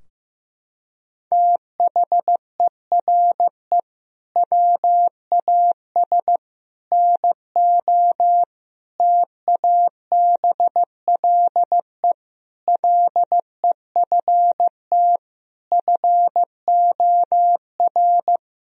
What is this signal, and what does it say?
Morse code, heard as a keyed tone, about 16 wpm, 700 Hz. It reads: THERE WAS NO TABLE LEFT FOR
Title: Sentences from Top 1000 Words - Review Part 2 - 15wpm